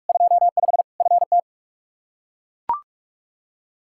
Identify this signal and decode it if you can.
Morse code, heard as a keyed tone, about 41 wpm, 700 Hz. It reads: 25 FT
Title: Practice QSO - Exchange 2 - Info - 45wpm